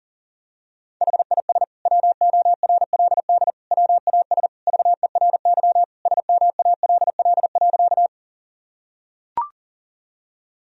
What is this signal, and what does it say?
Morse code, heard as a keyed tone, about 37 wpm, 700 Hz. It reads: HIS WORLD WAS VERY SMALL.
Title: Sentences from Top 1000 Words - Review Part 1 - 40wpm